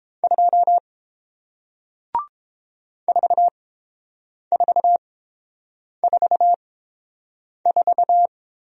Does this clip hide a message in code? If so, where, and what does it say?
Yes: Morse code from 0.2 to 8.3 s, heard as a keyed tone, about 30 wpm, spaced as for 7 wpm, 700 Hz. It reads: 2 4444